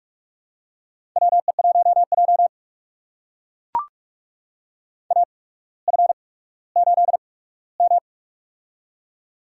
Morse code, heard as a keyed tone, about 44 wpm, spaced as for 19 wpm, 690 Hz. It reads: WE1J AF8M
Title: USA Call Signs - 4 Characters - 8x Character Spacing - 45wpm